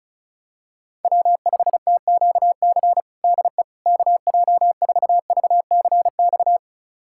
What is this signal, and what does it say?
Morse code, heard as a keyed tone, about 34 wpm, 690 Hz. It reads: W5TQC DE KJ4VCX
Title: USA Callsign DE Callsign 35wpm